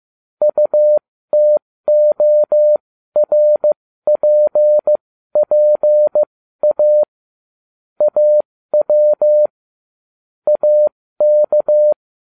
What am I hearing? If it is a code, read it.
Morse code, heard as a keyed tone, about 15 wpm, spaced as for 13 wpm, 600 Hz. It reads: UTORPPA AW AK